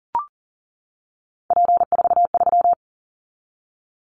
Morse code, heard as a keyed tone, about 40 wpm, 700 Hz. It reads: P43
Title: Callsign Prefixes for Top 100 DXCC - 40wpm